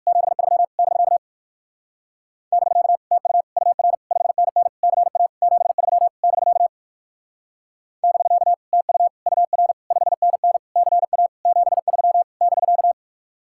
Morse code, heard as a keyed tone, about 56 wpm, 700 Hz. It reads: 73 <BK> <BK> TU UR 5NN CA 73 <BK> <BK> TU UR 5NN CA 73 <BK>
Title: Speed-Racing Common QSO Phrases - 45wpm